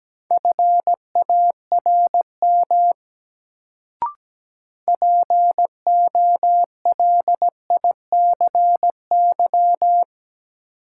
Morse code, heard as a keyed tone, about 17 wpm, 700 Hz. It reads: FARM POLICY
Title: Sentences from Top 1200 Words - New Word Drill Alt - 17wpm